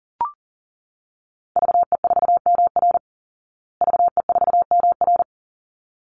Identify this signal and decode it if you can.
Morse code, heard as a keyed tone, about 40 wpm, 700 Hz. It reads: VE4MR VE4MR
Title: Call Signs Repeated 3x - 40wpm